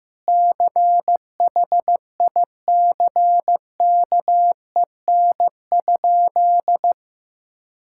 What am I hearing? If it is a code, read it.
Morse code, heard as a keyed tone, about 15 wpm, 700 Hz. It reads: CHICKEN?
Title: Sentences from Top 1000 Words - Review Part 5 - 15wpm